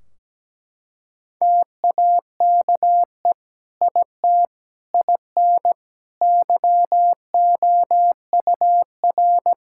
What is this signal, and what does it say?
Morse code, heard as a keyed tone, about 17 wpm, 700 Hz. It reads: TAKE IT IN YOUR
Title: Sentences from Top 1600 Words 17wpm